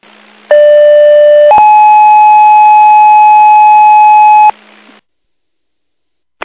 Two-tone page, heard as a one-second tone, then a three-second tone, about 590 then 850 Hz. It is heard from 0.5 to 4.5 s.